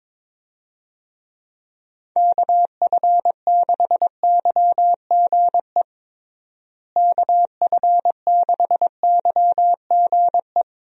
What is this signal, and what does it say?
Morse code, heard as a keyed tone, about 22 wpm, 700 Hz. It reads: KF6YGE KF6YGE
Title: USA Call Signs - 6 Characters - Repeated 2x - 22wpm